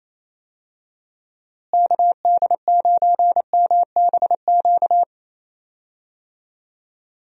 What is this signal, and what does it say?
Morse code, heard as a keyed tone, about 27 wpm, 700 Hz. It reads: KD9MBQ
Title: USA Call Signs - 6 Characters - Repeated 2x - 28wpm